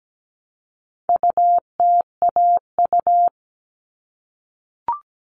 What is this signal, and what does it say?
Morse code, heard as a keyed tone, about 17 wpm, 700 Hz. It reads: UTAU
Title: All Cut Numbers - 4-Digit 17wpm